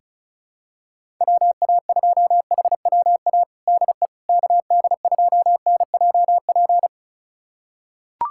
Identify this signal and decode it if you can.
Morse code, heard as a keyed tone, about 34 wpm, 700 Hz. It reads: WA2HWA DE KD2NJP
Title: USA Callsign DE Callsign - 6 Characters - 35wpm